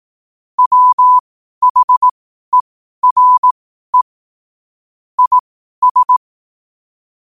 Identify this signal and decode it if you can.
Morse code, heard as a keyed tone, about 18 wpm, spaced as for 12 wpm, 1000 Hz. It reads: WHERE IS
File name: qotdfiles / 12wpm